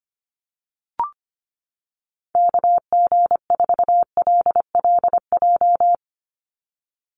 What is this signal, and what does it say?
Morse code, heard as a keyed tone, about 25 wpm, 700 Hz. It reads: KG4LLJ
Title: USA Call Signs 25wpm